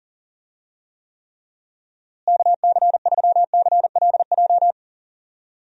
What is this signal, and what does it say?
Morse code, heard as a keyed tone, about 38 wpm, 700 Hz. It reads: KC3CLJ
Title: USA Call Signs - 6 Characters - Repeated 3x - 40wpm